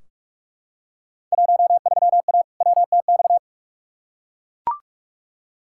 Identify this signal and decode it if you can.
Morse code, heard as a keyed tone, about 41 wpm, 700 Hz. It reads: 13A WTX
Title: Field Day Exchanges Encore - 45wpm